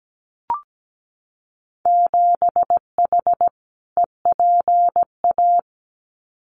Morse code, heard as a keyed tone, about 17 wpm, 700 Hz. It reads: 7H EPA
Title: Winter Field Day Exchanges Encore 17wpm